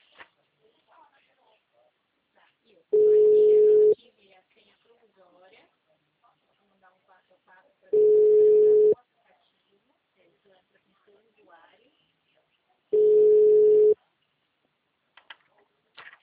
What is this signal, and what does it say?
Ringback tone (Europe, 425 Hz), with speech.